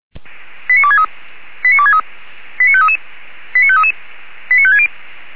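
Five-tone selective call (ZVEI), heard as quick runs of five tones, tones of about 70 ms. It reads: 89263 89263 89533 89533 89560
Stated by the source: ZVEI five-tone selective call received with an SDR.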